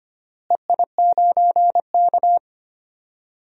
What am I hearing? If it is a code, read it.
Morse code, heard as a keyed tone, about 25 wpm, 700 Hz. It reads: EI9K
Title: DXCC-100 Call Signs - 4 Characters - Repeated 3x - 25wpm